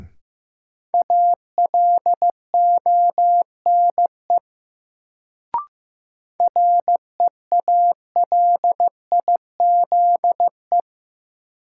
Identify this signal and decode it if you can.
Morse code, heard as a keyed tone, about 15 wpm, 700 Hz. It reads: ALONE REALIZE
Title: Sentences from Top 1000 Words - New Word Drill - 15wpm